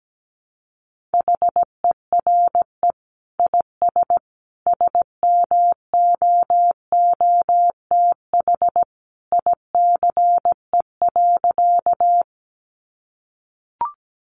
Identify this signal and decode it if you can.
Morse code, heard as a keyed tone, about 17 wpm, 700 Hz. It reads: HERE IS SMOOTH ICE.
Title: Sentences from Top 1400 Words 17wpm